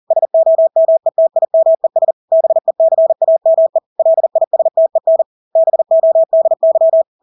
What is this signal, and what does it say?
Morse code, heard as a keyed tone, about 40 wpm, 650 Hz. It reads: SOMETIMES BECAME LISTEN BODY
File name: Engelske ord 40-40 wpm